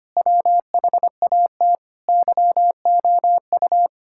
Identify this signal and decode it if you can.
Morse code, heard as a keyed tone, about 25 wpm, 700 Hz. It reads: WHAT YOU
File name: practice / 25wpm / Sentences with 9-Word Phrases - 25wpm